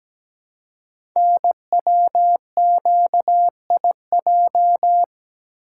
Morse code, heard as a keyed tone, about 17 wpm, 700 Hz. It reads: NWQIJ
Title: Code Groups - 5 Letter:Numbers - 17wpm